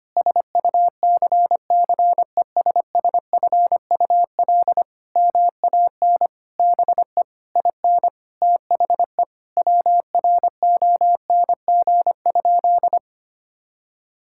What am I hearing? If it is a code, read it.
Morse code, heard as a keyed tone, about 25 wpm, 700 Hz. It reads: SUCCESSFUL MAN BE IN THE WRONG?
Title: Sentences from Top 1000 Words - Review Part 6 - 25wpm